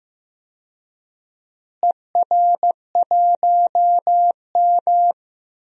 Morse code, heard as a keyed tone, about 15 wpm, 700 Hz. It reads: ER1M
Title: DXCC-100 Call Signs - 4 Characters - Repeated 3x - 15wpm